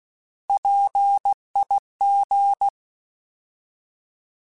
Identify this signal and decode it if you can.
Morse code, heard as a keyed tone, about 16 wpm, 800 Hz. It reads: PIG
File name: Alphabetical / Alphabetical mp3s / Ps @15